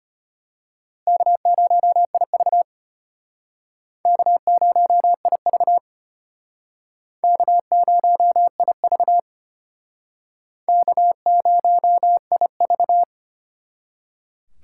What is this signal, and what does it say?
Morse code, heard as a keyed tone, about 32 wpm, 700 Hz. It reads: K0IV K0IV K0IV K0IV